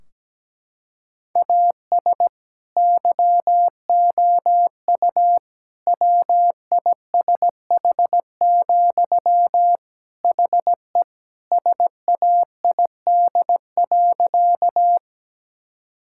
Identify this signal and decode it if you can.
Morse code, heard as a keyed tone, about 17 wpm, 700 Hz. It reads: AS YOU WISH, HE SAID.
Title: Sentences from Top 1600 Words 17wpm